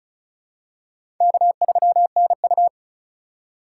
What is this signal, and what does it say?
Morse code, heard as a keyed tone, about 36 wpm, 700 Hz. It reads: K3NU
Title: USA Call Signs - 4 Characters - 2x Character Spacing - 35wpm